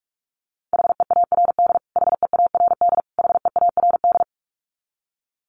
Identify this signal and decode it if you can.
Morse code, heard as a keyed tone, about 44 wpm, 700 Hz. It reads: HEARD HEARD HEARD
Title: Top 500 Words Repeated 45wpm